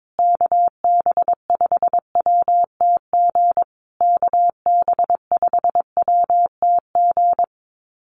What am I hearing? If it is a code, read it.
Morse code, heard as a keyed tone, about 22 wpm, 700 Hz. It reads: KB5WTG KB5WTG
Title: USA Call Signs Repeated 3x 22wpm